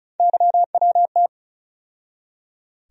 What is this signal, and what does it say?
Morse code, heard as a keyed tone, about 36 wpm, 700 Hz. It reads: YWT
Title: USA Call Sign Postfixes - 3 Letters 35wpm